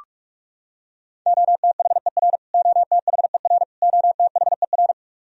Morse code, heard as a keyed tone, about 46 wpm, 700 Hz. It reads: OTHER OTHER OTHER
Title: Top 100 Words Repeated 45wpm